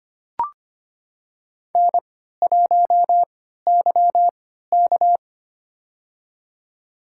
Morse code, heard as a keyed tone, about 25 wpm, spaced as for 14 wpm, 700 Hz. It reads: N1YK
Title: USA Call Signs - 4 Characters - 3x Character Spacing - 25wpm